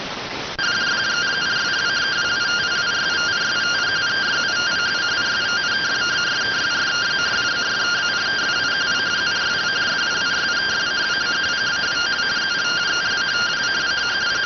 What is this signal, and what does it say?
Two-tone teleprinter signal, about 170 Hz shift.